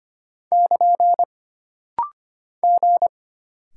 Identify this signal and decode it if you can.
Morse code, heard as a keyed tone, about 25 wpm, 700 Hz. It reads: <KN> G
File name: Single Character (Letter:Number:Punctuation:Prosign) - Rapid-Fire - 25wpm